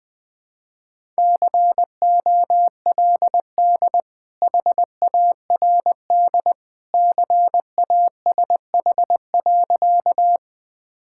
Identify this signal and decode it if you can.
Morse code, heard as a keyed tone, about 21 wpm, 700 Hz. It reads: COLD HARD CASH.
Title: Sentences from Top 1000 Words - Review Part 5 - 20wpm